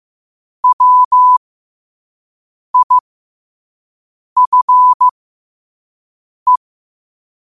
Morse code, heard as a keyed tone, about 15 wpm, spaced as for 5 wpm, 1000 Hz. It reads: WIFE
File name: qotdfiles / 5wpm